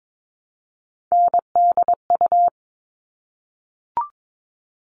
Morse code, heard as a keyed tone, about 22 wpm, 700 Hz. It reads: NDU
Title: All Cut Numbers - 3-Digit 22wpm